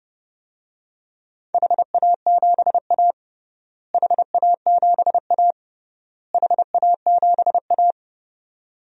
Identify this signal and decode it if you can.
Morse code, heard as a keyed tone, about 30 wpm, 700 Hz. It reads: HA7A HA7A HA7A
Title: DXCC-100 Call Signs - 4 Characters - Repeated 3x - 30wpm